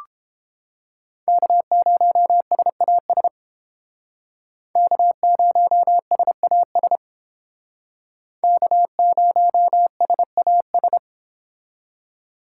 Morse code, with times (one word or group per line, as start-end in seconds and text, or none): 1.3-3.3 K0SAS
4.8-7.0 K0SAS
8.4-11.0 K0SAS